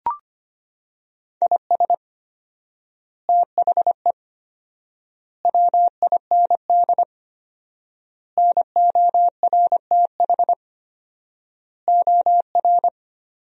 Morse code, heard as a keyed tone, about 25 wpm, 700 Hz. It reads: IS THE WIND NORTH OR
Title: 4x Word-Spacing - Sentences from Top 400 Words - 25wpm